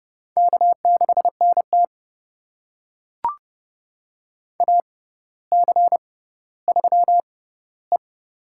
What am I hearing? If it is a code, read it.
Morse code, heard as a keyed tone, about 30 wpm, spaced as for 15 wpm, 700 Hz. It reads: K6NT AC3E
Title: USA Call Signs - 4 Characters - 6x Character Spacing - 30wpm